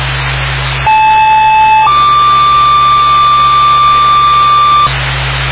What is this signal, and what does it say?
Two-tone page, heard as a one-second tone, then a three-second tone, about 870 then 1180 Hz, with noise. It is heard from 0.9 to 4.9 s.